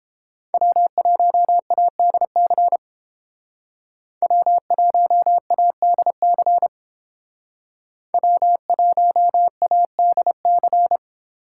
Morse code, heard as a keyed tone, about 28 wpm, 700 Hz. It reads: W1ADC W1ADC W1ADC